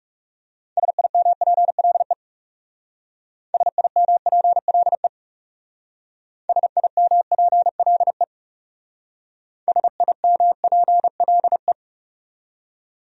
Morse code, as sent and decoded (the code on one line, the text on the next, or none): ... .. -- .--. .-.. / ... .. -- .--. .-.. / ... .. -- .--. .-.. / ... .. -- .--. .-.. .
SIMPLE SIMPLE SIMPLE SIMPLE